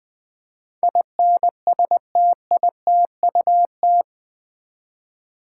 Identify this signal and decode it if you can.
Morse code, heard as a keyed tone, about 20 wpm, 700 Hz. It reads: INSTITUT
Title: Build A Word - 4 Letters At A Time - 11-Letter Words - 20wpm